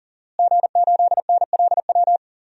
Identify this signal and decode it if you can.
Morse code, heard as a keyed tone, about 40 wpm, 700 Hz. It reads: G8NLW
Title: DXCC-100 Call Signs - 5 Characters - Repeated 2x - 40wpm